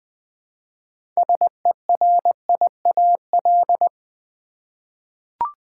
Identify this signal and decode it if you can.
Morse code, heard as a keyed tone, about 20 wpm, 700 Hz. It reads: SERIAL
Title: 6 Letter Words - First 3 Letters Given - 20wpm